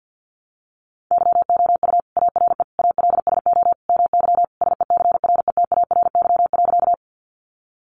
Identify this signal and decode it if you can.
Morse code, heard as a keyed tone, about 50 wpm, 700 Hz. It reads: YOU ARE ALSO MY SECRETARY.
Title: Sentences from Top 1300 Words 50wpm